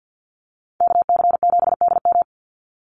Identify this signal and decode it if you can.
Morse code, heard as a keyed tone, about 48 wpm, 700 Hz. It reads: KC7DM
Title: NA SOTA Activator Callsigns 50wpm